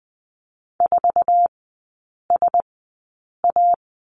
Morse code, heard as a keyed tone, about 20 wpm, spaced as for 8 wpm, 700 Hz. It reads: 4SA
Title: 060 Lesson 11e - Letter h - Three Characters 20wpm